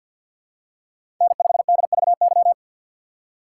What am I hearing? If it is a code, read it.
Morse code, heard as a keyed tone, about 47 wpm, 700 Hz. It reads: N5DVY
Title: USA Call Signs - 4x Character Spacing - 50wpm